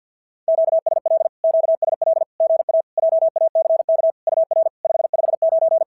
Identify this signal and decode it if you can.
Morse code, heard as a keyed tone, about 51 wpm, 690 Hz. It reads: QSL QSL GA JACK UR 559
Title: SOTA QSO Practice 50wpm